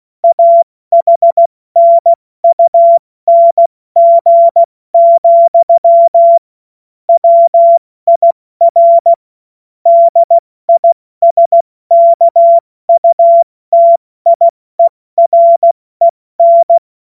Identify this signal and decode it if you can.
Morse code, heard as a keyed tone, about 16 wpm, 680 Hz. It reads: AHNUNG, WIR DISKUTIEREN
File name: Morsekurs / Witze / Witz 10.Oktober 2023 70 F80